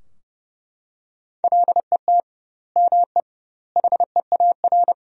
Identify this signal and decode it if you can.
Morse code, heard as a keyed tone, about 32 wpm, 700 Hz. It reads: LET ME HEAR
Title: 2x Word-Spacing - Sentences from Top 400 Words - 30wpm